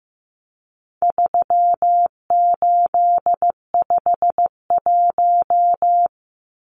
Morse code, heard as a keyed tone, about 15 wpm, 700 Hz. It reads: 3851
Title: Numbers - 4-Digit 15wpm